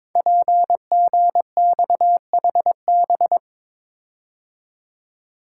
Morse code, heard as a keyed tone, about 22 wpm, 700 Hz. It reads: PGXHB
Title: Code Groups - 5 Letters - 22wpm